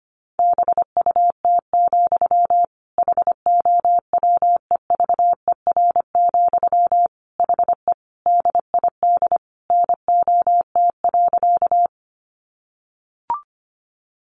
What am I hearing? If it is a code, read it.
Morse code, heard as a keyed tone, about 25 wpm, 700 Hz. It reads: BUT, HOWEVER, HE DID NOT.
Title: Sentences from Top 600 Words 25wpm